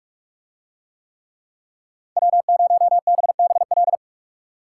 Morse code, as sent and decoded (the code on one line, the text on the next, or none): .-- ----- -... -... .-..
W0BBL